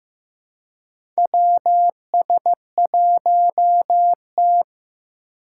Morse code, heard as a keyed tone, about 15 wpm, 700 Hz. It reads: WS1T
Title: USA Call Signs - 4 Characters - 4x Character Spacing - 15wpm